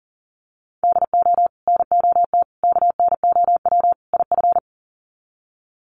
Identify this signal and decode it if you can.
Morse code, heard as a keyed tone, about 40 wpm, 700 Hz. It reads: DO NOT KNOW IF
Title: Sets of 4 Words 40wpm